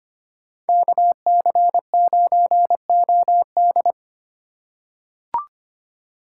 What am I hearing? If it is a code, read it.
Morse code, heard as a keyed tone, about 25 wpm, 700 Hz. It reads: KC9OD
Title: USA Call Signs - 5 Characters - 2x Character Spacing - 25wpm